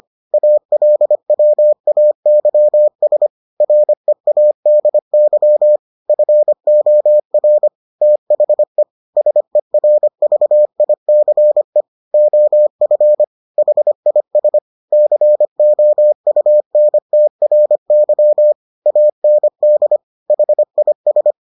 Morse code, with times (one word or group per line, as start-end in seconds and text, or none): 0.3-3.3 ALWAYS
3.6-5.8 READY
6.1-7.7 FOR
8.0-8.8 THE
9.2-11.8 SERVICE
12.1-13.2 OF
13.6-14.6 HIS
14.9-18.5 COUNTRY
18.9-20.0 AND
20.3-21.3 HIS